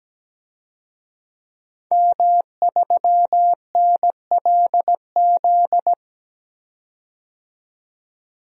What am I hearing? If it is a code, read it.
Morse code, heard as a keyed tone, about 17 wpm, 700 Hz. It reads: M3NLZ
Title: DXCC-100 Call Signs - 5 Characters - 17wpm